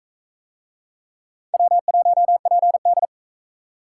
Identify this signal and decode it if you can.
Morse code, heard as a keyed tone, about 40 wpm, 700 Hz. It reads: W1PD